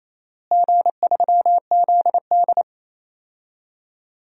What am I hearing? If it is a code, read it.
Morse code, heard as a keyed tone, about 28 wpm, 700 Hz. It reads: G3ZD
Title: DXCC-100 Call Signs - 4 Characters - Repeated 3x - 28wpm